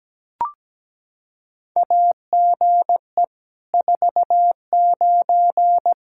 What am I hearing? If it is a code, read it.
Morse code, heard as a keyed tone, about 17 wpm, 700 Hz. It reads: AGE 49
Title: K1USN SST QSO Practice 17wpm